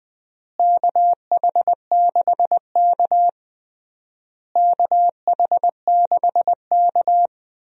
Morse code, heard as a keyed tone, about 20 wpm, 700 Hz. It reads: KH6K KH6K
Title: DXCC-100 Call Signs - 4 Characters - Repeated 3x - 20wpm